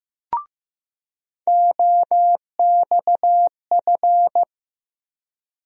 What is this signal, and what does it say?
Morse code, heard as a keyed tone, about 15 wpm, 700 Hz. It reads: OXF
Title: USA Call Sign Postfixes - 3 Letters 15wpm